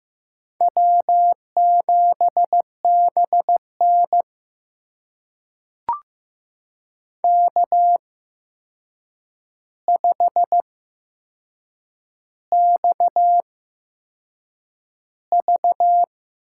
Morse code, heard as a keyed tone, about 15 wpm, spaced as for 6 wpm, 700 Hz. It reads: W7BN K5XV
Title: USA Call Signs - 8x Character Spacing - 15wpm